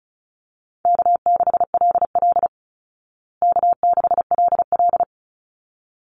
Morse code, heard as a keyed tone, about 34 wpm, 700 Hz. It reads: K6LL K6LL
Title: Easy Call Signs Repeated 2x - 35wpm